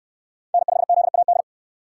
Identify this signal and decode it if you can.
Morse code, heard as a keyed tone, about 63 wpm, 700 Hz. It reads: N5ZAD